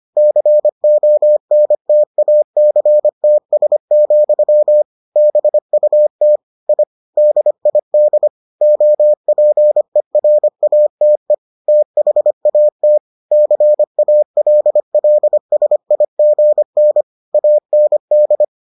Morse code, heard as a keyed tone, about 25 wpm, 600 Hz. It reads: CONTACTS, BUT I DID OPERATE THAT CALLSIGN AND